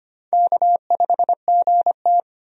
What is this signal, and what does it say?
Morse code, heard as a keyed tone, about 25 wpm, 700 Hz. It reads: K5GT
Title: USA Call Signs - 4 Characters - 25wpm